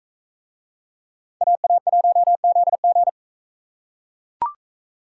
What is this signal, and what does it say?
Morse code, heard as a keyed tone, about 39 wpm, 700 Hz. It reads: AA1ZG